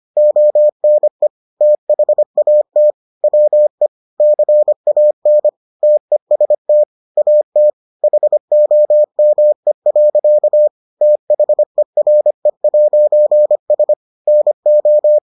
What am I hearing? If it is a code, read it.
Morse code, heard as a keyed tone, about 25 wpm, 600 Hz. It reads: ONE THAT WE CAN TEST AT HOME. THERE'S NO